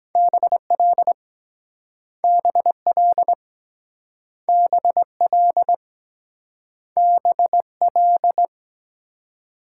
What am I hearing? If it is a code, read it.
Morse code, heard as a keyed tone, about 24 wpm, 700 Hz. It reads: BL BL BL BL